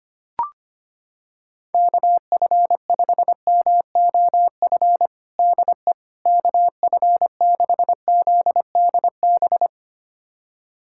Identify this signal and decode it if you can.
Morse code, heard as a keyed tone, about 25 wpm, 700 Hz. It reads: KF5MOF DE KF6ZDB
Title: USA Callsign DE Callsign - 6 Characters - 25wpm